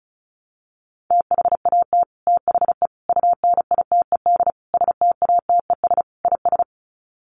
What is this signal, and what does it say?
Morse code, heard as a keyed tone, about 34 wpm, 700 Hz. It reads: THAT THE UNITED STATES IS
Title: Sets of 5 Words Encore 35wpm